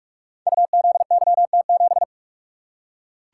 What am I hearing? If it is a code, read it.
Morse code, heard as a keyed tone, about 47 wpm, 700 Hz. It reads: UZYT7
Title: Code Groups - 5 Letter:Numbers - 45wpm